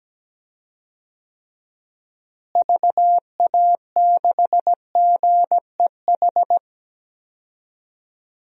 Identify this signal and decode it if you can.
Morse code, heard as a keyed tone, about 17 wpm, 700 Hz. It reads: VA6GEH
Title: DXCC-100 Call Signs - 6 Characters - Repeated 3x - 17wpm